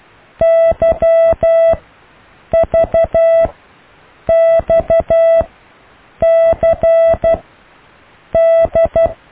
Morse code, heard as a keyed tone, about 12 wpm, spaced as for 7 wpm, 660 Hz. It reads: YVXCD
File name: cw / Alle letters